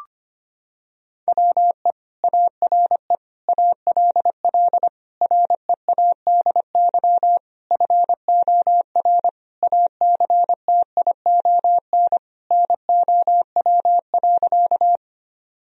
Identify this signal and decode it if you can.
Morse code, heard as a keyed tone, about 25 wpm, 700 Hz. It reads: WE ARE ALL READY FOR ACTION NOW.
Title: Sentences from Top 1000 Words - Review Part 4 - 25wpm